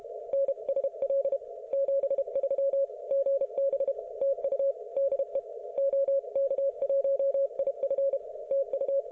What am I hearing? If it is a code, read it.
Morse code, heard as a keyed tone, about 32 wpm, 550 Hz. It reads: NSL 73 GB TU DE OK1IF TU